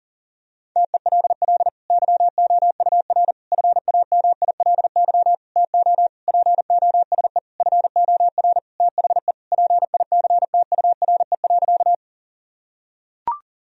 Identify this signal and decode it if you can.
Morse code, heard as a keyed tone, about 40 wpm, 700 Hz. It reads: TELL YOUR FAMILY TO POSE FOR THE PICTURE.
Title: Sentences from Top 1000 Words - Review Part 2 - 40wpm